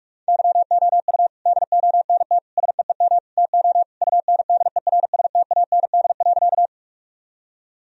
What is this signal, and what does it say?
Morse code, heard as a keyed tone, about 46 wpm, 700 Hz. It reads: YOU DONT SEEM TO UNDERSTAND.
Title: Sentences from Top 1000 Words - Review Part 1 - 45wpm